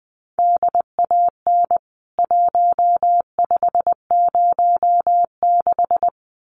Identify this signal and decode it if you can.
Morse code, heard as a keyed tone, about 20 wpm, 700 Hz. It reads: DAN 1506
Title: CWT Exchange 20wpm